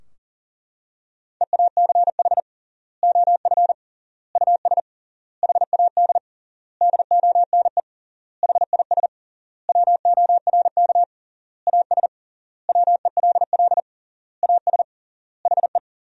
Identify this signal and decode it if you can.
Morse code, heard as a keyed tone, about 40 wpm, 700 Hz. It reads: EACH OF US HAD DONE HIS WORK AS WELL AS HE
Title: 3x Word-Spacing - Sentences from Top 500 Words - 40wpm